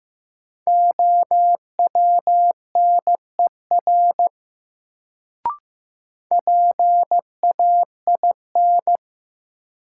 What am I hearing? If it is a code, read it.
Morse code, heard as a keyed tone, about 15 wpm, 700 Hz. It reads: OWNER PAIN
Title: Sentences from Top 1600 Words - New Word Drill - 15wpm